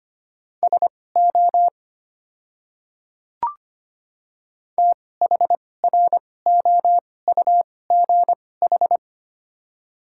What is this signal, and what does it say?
Morse code, heard as a keyed tone, about 25 wpm, spaced as for 18 wpm, 700 Hz. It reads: SO THROUGH
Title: 2x Character Spacing - Top 200 Words - 25wpm